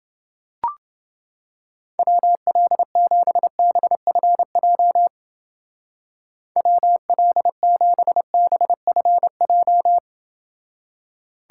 Morse code, heard as a keyed tone, about 28 wpm, 700 Hz. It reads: WL7BFJ WL7BFJ